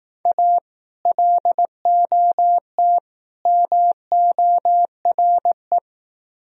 Morse code, heard as a keyed tone, about 19 wpm, 700 Hz. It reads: A LOT MORE